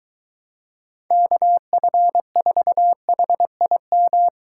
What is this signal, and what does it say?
Morse code, heard as a keyed tone, about 22 wpm, 700 Hz. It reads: KF4HIM